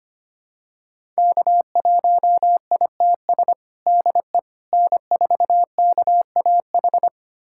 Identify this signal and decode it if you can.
Morse code, heard as a keyed tone, about 25 wpm, 700 Hz. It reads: K1ITS DE N4KAH
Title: USA Callsign DE Callsign - 5 Characters - 25wpm